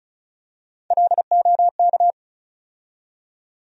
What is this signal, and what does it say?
Morse code, heard as a keyed tone, about 36 wpm, 700 Hz. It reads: LOK
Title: USA Call Sign Postfixes - 3 Letters 35wpm